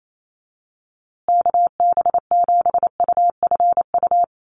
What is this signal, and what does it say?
Morse code, heard as a keyed tone, about 28 wpm, 700 Hz. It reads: KB7UFU
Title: USA Call Signs Repeated 3x 28wpm